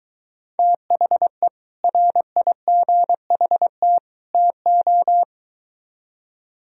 Morse code, heard as a keyed tone, about 23 wpm, 700 Hz. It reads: THE RIGHT TO